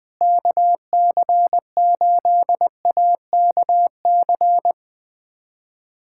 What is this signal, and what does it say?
Morse code, heard as a keyed tone, about 21 wpm, 700 Hz. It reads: KC8AKC
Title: USA Callsign DE Callsign - 6 Characters - 20wpm